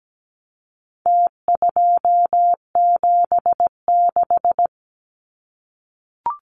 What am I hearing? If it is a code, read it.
Morse code, heard as a keyed tone, about 17 wpm, 700 Hz. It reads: T276
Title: Numbers - Contest Style - 4-Digit 17wpm